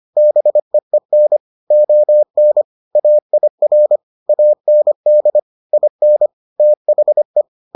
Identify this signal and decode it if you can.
Morse code, heard as a keyed tone, about 25 wpm, 600 Hz. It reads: BEEN ON AIR AND IN THE